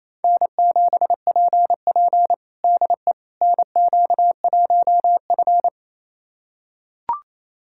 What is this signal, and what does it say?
Morse code, heard as a keyed tone, about 28 wpm, 700 Hz. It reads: N7PP DE NQ1F
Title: USA Callsign DE Callsign - 4 Characters - 28wpm